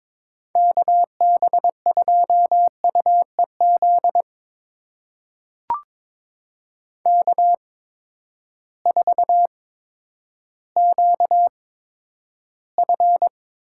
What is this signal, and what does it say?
Morse code, heard as a keyed tone, about 21 wpm, spaced as for 11 wpm, 700 Hz. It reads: KB2UEZ K4QF
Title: USA Call Signs - 8x Character Spacing - 22wpm